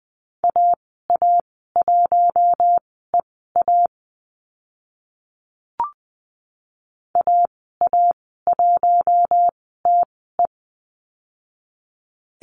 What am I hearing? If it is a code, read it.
Morse code, heard as a keyed tone, about 20 wpm, spaced as for 14 wpm, 700 Hz. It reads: AA1EA AA1TE
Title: 042 Lesson 8g - Number 1 - Call Signs 20wpm